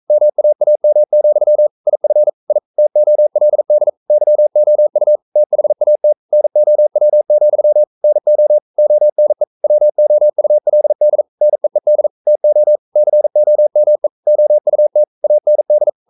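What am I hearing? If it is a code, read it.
Morse code, heard as a keyed tone, about 42 wpm, 600 Hz. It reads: MAAM, IF I TOLD YOU THAT NOW, NO ONE WOULD NEED TO COME OUT AND
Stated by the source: Morse code practice at 41 wpm, from CW med Gnister.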